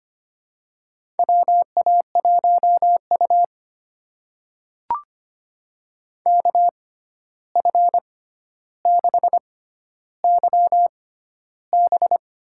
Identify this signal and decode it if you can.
Morse code, heard as a keyed tone, about 25 wpm, spaced as for 12 wpm, 700 Hz. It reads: WA1U KF6YB
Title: USA Call Signs - 6x Character Spacing - 25wpm